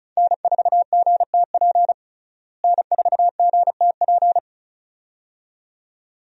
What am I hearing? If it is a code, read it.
Morse code, heard as a keyed tone, about 36 wpm, 700 Hz. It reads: N4GTP N4GTP
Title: USA Call Signs - 5 Characters - Repeated 2x - 35wpm